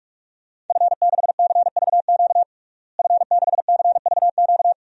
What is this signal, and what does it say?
Morse code, heard as a keyed tone, about 46 wpm, 700 Hz. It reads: F6CVQ F6CVQ
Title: DXCC-100 Call Signs - 5 Characters - Repeated 3x - 45wpm